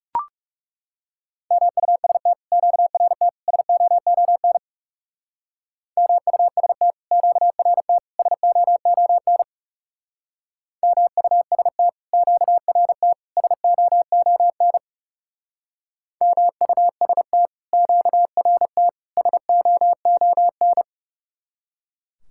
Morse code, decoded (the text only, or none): MUST QRT SOON MUST QRT SOON MUST QRT SOON MUST QRT SOON